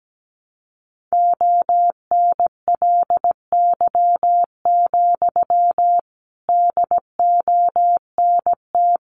Morse code, heard as a keyed tone, about 17 wpm, 700 Hz. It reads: ONLY, DONT
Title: Sentences from Top 300 Words 17wpm